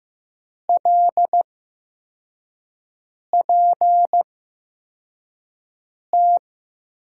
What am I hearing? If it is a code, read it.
Morse code, heard as a keyed tone, about 15 wpm, spaced as for 4 wpm, 700 Hz. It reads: LPT
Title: USA Call Signs - 6 Characters - 8x Character Spacing - 15wpm